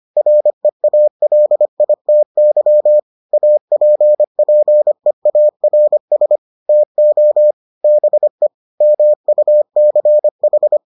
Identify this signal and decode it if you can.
Morse code, heard as a keyed tone, about 25 wpm, 600 Hz. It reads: REALITY APPEARS TO BE MUCH